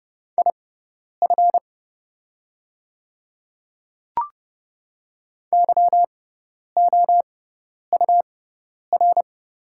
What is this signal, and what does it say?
Morse code, heard as a keyed tone, about 30 wpm, spaced as for 10 wpm, 700 Hz. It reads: IF YOUR
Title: 6x Character Spacing - Top 300 Words - 30wpm